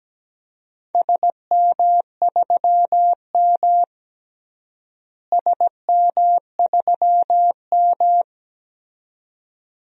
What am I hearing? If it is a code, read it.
Morse code, heard as a keyed tone, about 17 wpm, 700 Hz. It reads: SM3M SM3M
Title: DXCC-100 Call Signs - 4 Characters - Repeated 2x - 17wpm